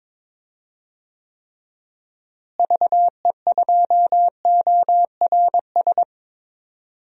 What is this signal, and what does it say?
Morse code, heard as a keyed tone, about 22 wpm, 700 Hz. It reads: VE2ORS
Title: DXCC-100 Call Signs - 6 Characters - Repeated 3x - 22wpm